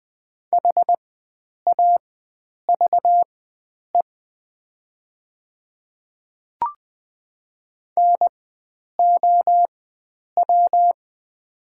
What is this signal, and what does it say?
Morse code, heard as a keyed tone, about 20 wpm, spaced as for 9 wpm, 700 Hz. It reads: HAVE NOW
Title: 4x Character Spacing - Top 100 Words - 20wpm